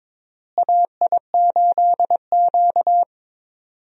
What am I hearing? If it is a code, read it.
Morse code, heard as a keyed tone, about 21 wpm, 700 Hz. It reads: AI8Q
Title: USA Call Signs - 4 Characters - Repeated 3x - 22wpm